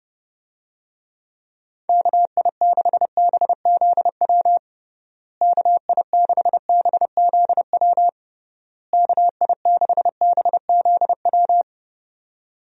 Morse code, heard as a keyed tone, about 30 wpm, 700 Hz. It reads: KI6BZW KI6BZW KI6BZW
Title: USA Call Signs - 6 Characters - Repeated 3x - 30wpm